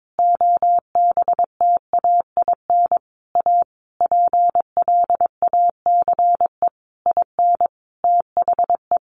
Morse code, heard as a keyed tone, about 22 wpm, 700 Hz. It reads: OBTAIN A PLACE IN THE
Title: Sentences from Top 1300 Words 22wpm